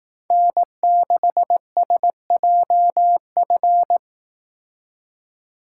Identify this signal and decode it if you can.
Morse code, heard as a keyed tone, about 18 wpm, 700 Hz. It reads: N6SJF
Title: Speed Racing - USA 1x3 Call Signs 15wpm